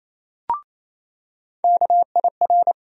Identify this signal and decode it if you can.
Morse code, heard as a keyed tone, about 28 wpm, 700 Hz. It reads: KIR
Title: USA Call Sign Postfixes - 3 Letters 28wpm